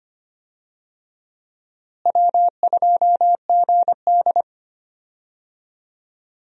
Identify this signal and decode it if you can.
Morse code, heard as a keyed tone, about 25 wpm, 700 Hz. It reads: W2GD
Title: USA Call Signs - 4 Characters - Repeated 2x - 25wpm